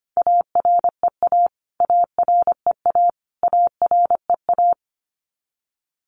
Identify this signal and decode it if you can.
Morse code, heard as a keyed tone, about 25 wpm, 700 Hz. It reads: AREA AREA AREA
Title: Top 500 Words Repeated 25wpm